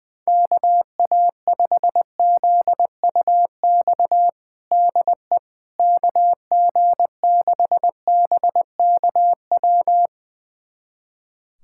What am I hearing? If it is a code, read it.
Morse code, heard as a keyed tone, about 20 wpm, 700 Hz. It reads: KA5ZUX DE KG6BKW
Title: USA Callsign DE Callsign 20wpm